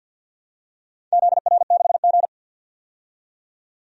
Morse code, heard as a keyed tone, about 51 wpm, 700 Hz. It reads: ZR6G
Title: DXCC-100 Call Signs - 4 Characters - 50wpm